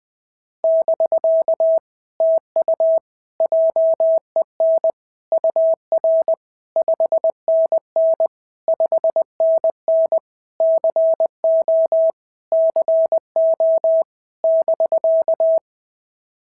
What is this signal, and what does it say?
Morse code, heard as a keyed tone, about 21 wpm, 660 Hz. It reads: <BK> TU JEN UR 5NN 5NN CO CO <BK>
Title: POTA QSO Practice - 20wpm